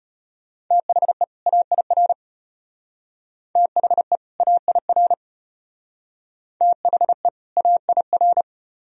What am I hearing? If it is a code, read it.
Morse code, heard as a keyed tone, about 32 wpm, 700 Hz. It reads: THE AIR THE AIR THE AIR